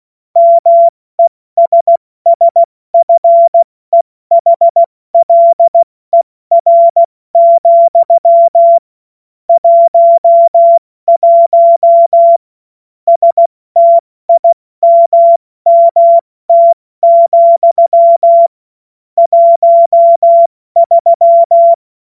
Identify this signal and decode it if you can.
Morse code, heard as a keyed tone, about 16 wpm, 680 Hz. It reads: MESSFEHLER, 11 STIMMT, 13
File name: Morsekurs / Witze / Witz 12. April 2022 70 F80